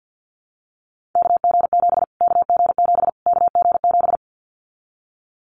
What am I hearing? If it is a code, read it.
Morse code, heard as a keyed tone, about 50 wpm, 700 Hz. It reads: KG7 KG7 KG7
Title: Callsign Prefixes for Top 25 DXCC - Repeated 3x - 50wpm